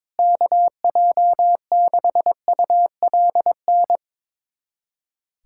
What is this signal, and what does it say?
Morse code, heard as a keyed tone, about 22 wpm, 700 Hz. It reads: KJ6ULN
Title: USA Callsign DE Callsign - 6 Characters - 22wpm